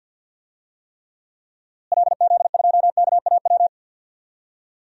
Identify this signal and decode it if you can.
Morse code, heard as a keyed tone, about 52 wpm, 700 Hz. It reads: LZ2KAW
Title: DXCC-100 Call Signs - 6 Characters - Repeated 2x - 50wpm